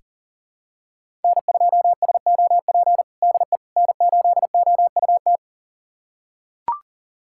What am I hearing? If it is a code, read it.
Morse code, heard as a keyed tone, about 37 wpm, 700 Hz. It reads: N2SOP DE N8OUT
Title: USA Callsign DE Callsign - 5 Characters - 40wpm